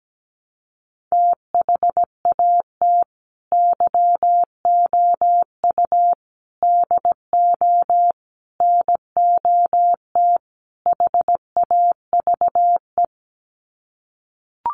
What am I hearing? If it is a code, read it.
Morse code, heard as a keyed tone, about 17 wpm, 700 Hz. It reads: THAT YOU DO NOT HAVE
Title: Sets of 5 Words Encore 17wpm